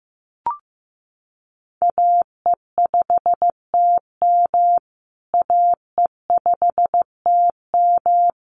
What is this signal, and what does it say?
Morse code, heard as a keyed tone, about 15 wpm, 700 Hz. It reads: AE5TM AE5TM
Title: USA Call Signs Repeated 2x 15wpm